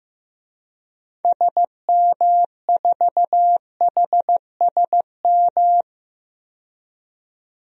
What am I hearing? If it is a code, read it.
Morse code, heard as a keyed tone, about 15 wpm, 700 Hz. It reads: SM4HSM
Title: DXCC-100 Call Signs - 6 Characters - 15wpm